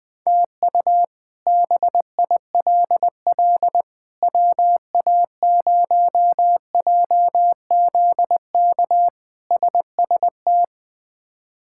Morse code, heard as a keyed tone, about 20 wpm, 700 Hz. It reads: TU BILL WA0JZK SST
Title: K1USN SST QSO Practice - 20wpm